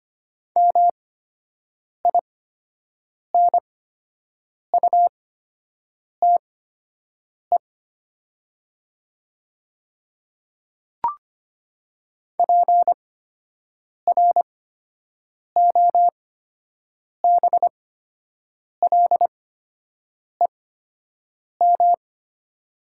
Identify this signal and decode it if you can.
Morse code, heard as a keyed tone, about 25 wpm, spaced as for 7 wpm, 700 Hz. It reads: MINUTE PROBLEM
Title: 8x Character Spacing - Top 500 Words - 25wpm